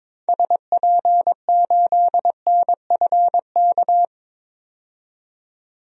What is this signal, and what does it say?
Morse code, heard as a keyed tone, about 22 wpm, 700 Hz. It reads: SP8NFK
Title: DXCC-100 Call Signs - 6 Characters - 22wpm